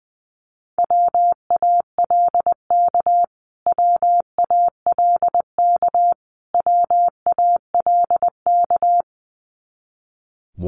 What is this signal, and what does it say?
Morse code, heard as a keyed tone, about 20 wpm, 700 Hz. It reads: WALK WALK WALK
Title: Top 500 Words Repeated 20wpm